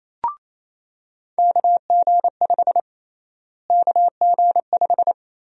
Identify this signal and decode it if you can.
Morse code, heard as a keyed tone, about 27 wpm, 700 Hz. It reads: KG5 KG5
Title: USA Call Sign Prefixes - Repeated 2x - 28wpm